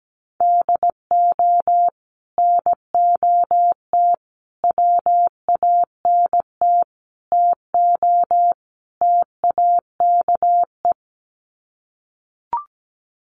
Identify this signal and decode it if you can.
Morse code, heard as a keyed tone, about 17 wpm, 700 Hz. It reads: DO NOT WANT TO TAKE
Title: Sets of 5 Words Encore2 17wpm